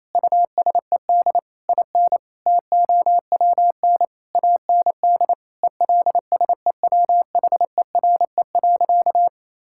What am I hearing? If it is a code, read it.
Morse code, heard as a keyed tone, about 28 wpm, 700 Hz. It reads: USED IN TOWN AND ELSEWHERE.
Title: Sentences from Top 1600 Words 28wpm